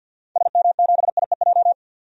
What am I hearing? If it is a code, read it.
Morse code, heard as a keyed tone, about 50 wpm, 700 Hz. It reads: SM7IEJ
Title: DXCC-100 Call Signs - 6 Characters - 50wpm